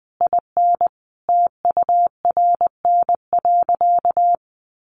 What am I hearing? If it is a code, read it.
Morse code, heard as a keyed tone, about 20 wpm, 700 Hz. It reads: IN TURN.
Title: Sentences from Top 1500 Words 20wpm